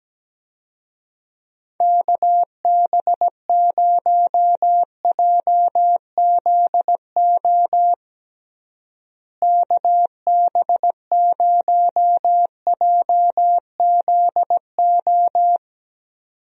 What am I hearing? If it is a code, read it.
Morse code, heard as a keyed tone, about 17 wpm, 700 Hz. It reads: KB0JZO KB0JZO
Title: USA Call Signs - 6 Characters - Repeated 2x - 17wpm